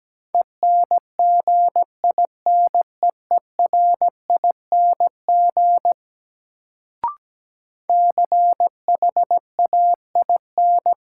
Morse code, heard as a keyed tone, about 17 wpm, 700 Hz. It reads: ENGINEERING CHAIN
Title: Sentences from Top 1500 Words - New Word Drill Alt - 17wpm